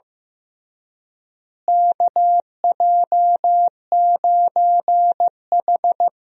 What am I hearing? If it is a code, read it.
Morse code, heard as a keyed tone, about 16 wpm, 700 Hz. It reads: KJ9H
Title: USA Call Signs - 4 Characters - Repeated 2x - 15wpm